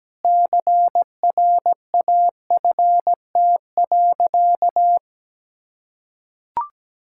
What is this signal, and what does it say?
Morse code, heard as a keyed tone, about 17 wpm, 700 Hz. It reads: CRAFT.
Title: Sentences from Top 1000 Words - Review Part 6 - 17wpm